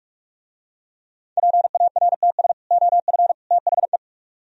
Morse code, heard as a keyed tone, about 45 wpm, 700 Hz. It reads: PARTS OF THE